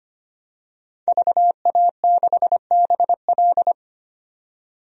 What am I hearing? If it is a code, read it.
Morse code, heard as a keyed tone, about 25 wpm, 690 Hz. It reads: VA6BL
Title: DXCC-100 Call Signs - 5 Characters - 25wpm